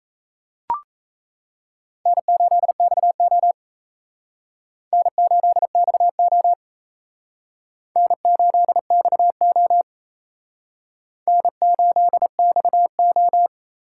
Morse code, heard as a keyed tone, about 36 wpm, 700 Hz. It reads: N8XO N8XO N8XO N8XO